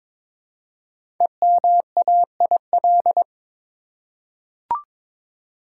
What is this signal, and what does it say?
Morse code, heard as a keyed tone, about 22 wpm, 690 Hz. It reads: EMAIL
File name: 5 Letter Words - First 3 Letters Given - 22wpm